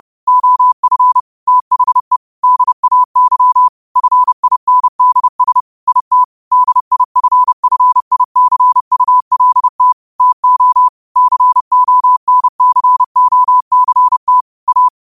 Morse code, heard as a keyed tone, about 30 wpm, 1000 Hz. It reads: OF THE DAY FINDS IT DIFFICULT TO CONCOCT A